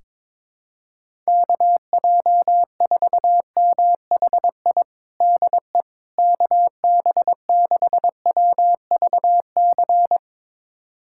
Morse code, heard as a keyed tone, about 22 wpm, 700 Hz. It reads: KJ4MHI DE KB6WVC
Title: USA Callsign DE Callsign - 6 Characters - 22wpm